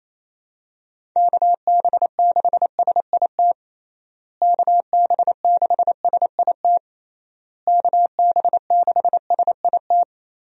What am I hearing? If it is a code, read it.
Morse code, heard as a keyed tone, about 28 wpm, 700 Hz. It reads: KB6SIT KB6SIT KB6SIT
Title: USA Call Signs - 6 Characters - Repeated 3x - 28wpm